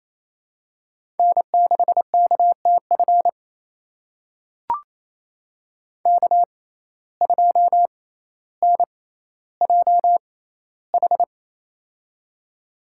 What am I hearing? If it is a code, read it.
Morse code, heard as a keyed tone, about 28 wpm, spaced as for 14 wpm, 700 Hz. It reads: N6KTF K2NJH
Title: USA Call Signs - 5 Characters - 6x Character Spacing - 28wpm